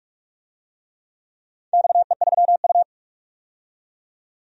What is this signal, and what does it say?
Morse code, heard as a keyed tone, about 45 wpm, 700 Hz. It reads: XE3U
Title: DXCC-100 Call Signs - 4 Characters - 45wpm